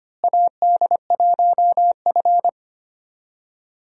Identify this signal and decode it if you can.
Morse code, heard as a keyed tone, about 25 wpm, 700 Hz. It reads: AD1F
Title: USA Call Signs - 4 Characters - Repeated 2x - 25wpm